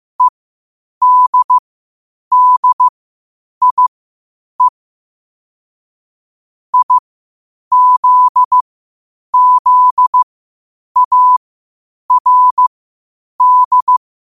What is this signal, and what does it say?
Morse code, heard as a keyed tone, about 15 wpm, spaced as for 8 wpm, 1000 Hz. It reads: EDDIE IZZARD
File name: qotdfiles / 8wpm